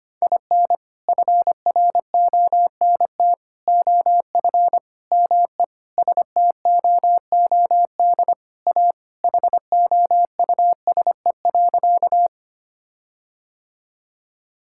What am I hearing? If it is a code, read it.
Morse code, heard as a keyed tone, about 25 wpm, 700 Hz. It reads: IN FRONT OF ME STOOD A HOUSE.
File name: practice / 25wpm / Sentences with 4-Word Phrases - 25wpm